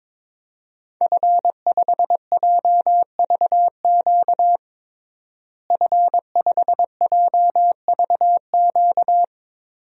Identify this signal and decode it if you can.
Morse code, heard as a keyed tone, about 22 wpm, 700 Hz. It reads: F5JVQ F5JVQ
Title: DXCC-100 Call Signs - 5 Characters - Repeated 2x - 22wpm